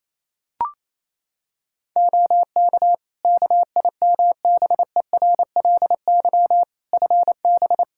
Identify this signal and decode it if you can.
Morse code, heard as a keyed tone, about 28 wpm, 700 Hz. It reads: OK KIMBERLY FB
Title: Practice QSO - Exchange 2 - Info - 28wpm